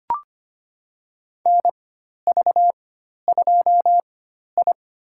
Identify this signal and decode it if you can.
Morse code, heard as a keyed tone, about 25 wpm, spaced as for 12 wpm, 700 Hz. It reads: NV2I
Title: USA Call Signs - 4x Character Spacing - 25wpm